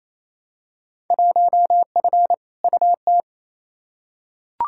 Morse code, heard as a keyed tone, about 28 wpm, 700 Hz. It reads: 1F UT
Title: Field Day Exchanges Encore - 28wpm